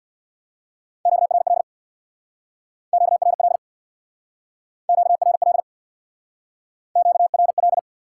Morse code, heard as a keyed tone, about 68 wpm, 700 Hz. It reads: QRL QRL QRL QRL